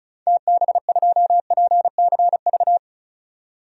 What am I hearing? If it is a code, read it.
Morse code, heard as a keyed tone, about 36 wpm, 700 Hz. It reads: TB2PCV
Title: DXCC-100 Call Signs - 6 Characters - 35wpm